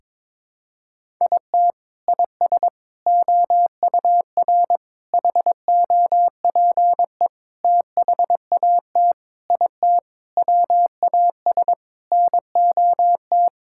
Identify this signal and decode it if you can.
Morse code, heard as a keyed tone, about 22 wpm, 700 Hz. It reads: IT IS OUR HOPE THAT IT WAS NOT
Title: Sentences from Top 1000 Words - Review Part 5 - 22wpm